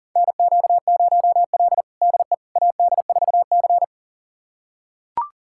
Morse code, heard as a keyed tone, about 40 wpm, 700 Hz. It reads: NQ0L DE AD4C
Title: USA Callsign DE Callsign - 4 Characters - 40wpm